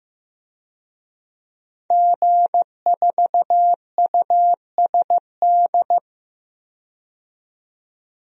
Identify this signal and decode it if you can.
Morse code, heard as a keyed tone, about 15 wpm, 700 Hz. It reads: G4USD
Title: DXCC-100 Call Signs - 5 Characters - 15wpm